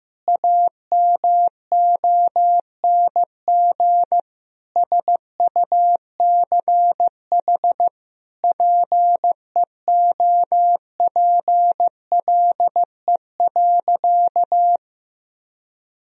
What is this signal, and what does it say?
Morse code, heard as a keyed tone, about 15 wpm, 700 Hz. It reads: AMONG SUCH PEOPLE.
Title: Sentences from Top 1000 Words - Review Part 3 - 15wpm